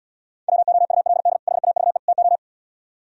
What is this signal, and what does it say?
Morse code, heard as a keyed tone, about 79 wpm, 700 Hz. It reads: WOULD HAVE TO